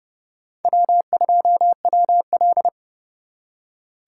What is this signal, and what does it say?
Morse code, heard as a keyed tone, about 38 wpm, 700 Hz. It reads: W2WL